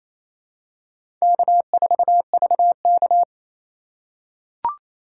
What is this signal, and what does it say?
Morse code, heard as a keyed tone, about 28 wpm, 700 Hz. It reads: K4VK
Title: USA Call Signs - 4 Characters - 2x Character Spacing - 28wpm